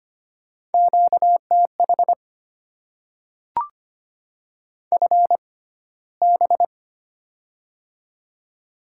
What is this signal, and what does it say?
Morse code, heard as a keyed tone, about 25 wpm, spaced as for 15 wpm, 690 Hz. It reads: QTH FB
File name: Top 20 QSO Elements - 6x Character Spacing - 25wpm